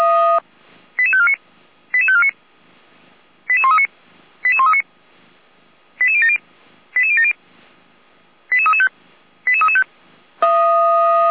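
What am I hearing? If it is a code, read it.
Five-tone selective call (ZVEI), heard as quick runs of five tones, tones of about 70 ms. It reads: 88549 88549 88139 88139 88070 88070 88335 88335
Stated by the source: ZVEI five-tone selective call received by SDR.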